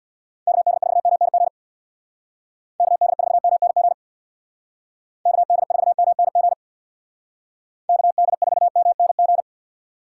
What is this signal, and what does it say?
Morse code, heard as a keyed tone, about 64 wpm, 700 Hz. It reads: KD4MNG KD4MNG KD4MNG KD4MNG